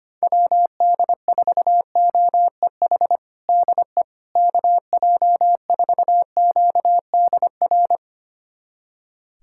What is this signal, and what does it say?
Morse code, heard as a keyed tone, about 25 wpm, 700 Hz. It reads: WD4OEH DE KJ4QDR
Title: USA Callsign DE Callsign - 6 Characters - 25wpm